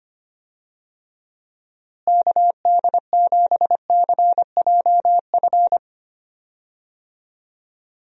Morse code, heard as a keyed tone, about 25 wpm, 700 Hz. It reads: KD7CJF
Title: USA Call Signs - 6 Characters - Repeated 3x - 25wpm